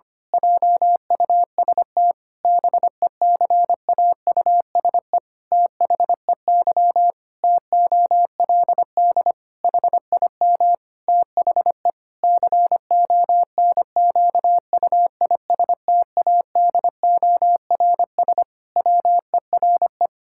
Morse code, heard as a keyed tone, about 25 wpm, 700 Hz. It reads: JUST BECAUSE THEY TOLD HIM THE CONQUISTADORS WERE
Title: Pangram Challenge v2 - 25wpm